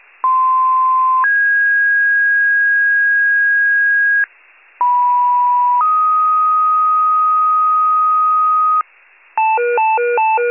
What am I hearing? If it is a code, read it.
Two-tone page, heard as a one-second tone, then a three-second tone, about 1040 then 1650 Hz, about 980 then 1250 Hz, with noise. It is heard from 0.2 to 8.8 s.